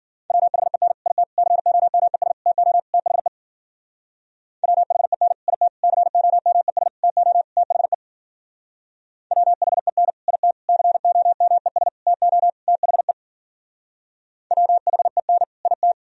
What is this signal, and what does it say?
Morse code, heard as a keyed tone, about 50 wpm, 700 Hz. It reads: WHEN IT COMES TO THE WHEN IT COMES TO THE WHEN IT COMES TO THE WHEN IT
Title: Speed-Racing - Sets of 5 Words 40wpm